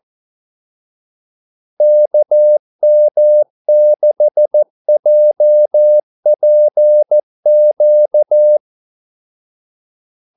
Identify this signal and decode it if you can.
Morse code, heard as a keyed tone, about 14 wpm, 600 Hz. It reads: KM6JPQ
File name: Callsigns-14wpm